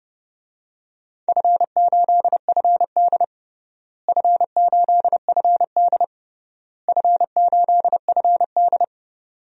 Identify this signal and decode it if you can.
Morse code, heard as a keyed tone, about 30 wpm, 700 Hz. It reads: F8FD F8FD F8FD
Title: DXCC-100 Call Signs - 4 Characters - Repeated 3x - 30wpm